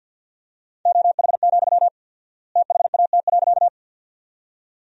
Morse code, heard as a keyed tone, about 47 wpm, 700 Hz. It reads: OH, THAT.
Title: 4x Word-Spacing - Sentences from Top 500 Words - 50wpm